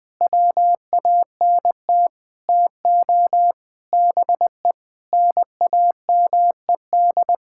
Morse code, heard as a keyed tone, about 20 wpm, 700 Hz. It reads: WANT TO BE NAMED
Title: Sentences with 8-Word Phrases - 20wpm